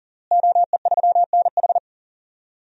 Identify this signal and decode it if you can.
Morse code, heard as a keyed tone, about 37 wpm, 700 Hz. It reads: OE3NH
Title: DXCC-100 Call Signs - 5 Characters - Repeated 2x - 40wpm